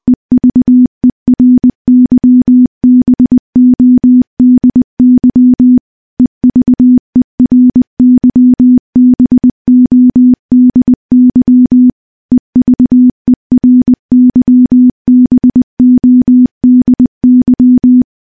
Morse code, heard as a keyed tone, about 21 wpm, 250 Hz. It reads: EVERYBODY EVERYBODY EVERYBODY
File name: morse-training-20WPM